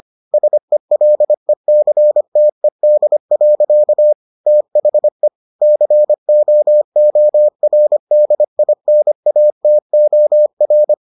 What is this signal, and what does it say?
Morse code, heard as a keyed tone, about 25 wpm, 600 Hz. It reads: SELECTED. THE COORDINATOR